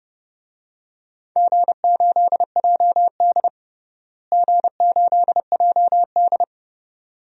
Morse code, heard as a keyed tone, about 30 wpm, 700 Hz. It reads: G8JD G8JD
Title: DXCC-100 Call Signs - 4 Characters - Repeated 2x - 30wpm